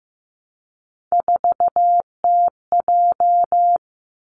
Morse code, heard as a keyed tone, about 15 wpm, 700 Hz. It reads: 4TJ
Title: Three Challenging Letter-Number Combinations 15wpm